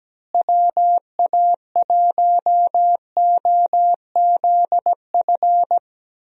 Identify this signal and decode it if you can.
Morse code, heard as a keyed tone, about 17 wpm, 700 Hz. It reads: WA1OZF
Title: USA Call Signs - 6 Characters - Repeated 3x - 17wpm